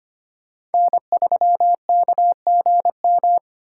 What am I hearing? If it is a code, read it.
Morse code, heard as a keyed tone, about 31 wpm, 700 Hz. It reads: N3KGM